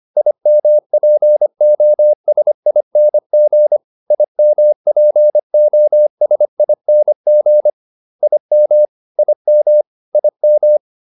Morse code, heard as a keyed tone, about 25 wpm, 600 Hz. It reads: IMPOSING IMPOSING IM IM IM